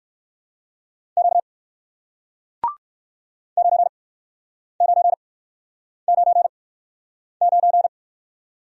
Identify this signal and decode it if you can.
Morse code, heard as a keyed tone, about 62 wpm, spaced as for 9 wpm, 700 Hz. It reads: Q 9999